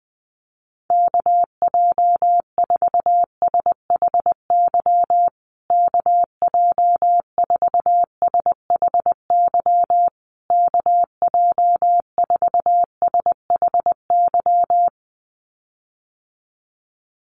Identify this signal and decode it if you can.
Morse code, heard as a keyed tone, about 20 wpm, 700 Hz. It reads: KJ4SHY KJ4SHY KJ4SHY
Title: USA Call Signs Repeated 3x 20wpm